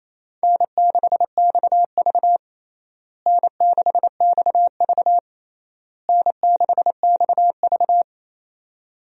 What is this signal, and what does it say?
Morse code, heard as a keyed tone, about 28 wpm, 700 Hz. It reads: N6XV N6XV N6XV
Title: USA Call Signs - 4 Characters - Repeated 3x - 28wpm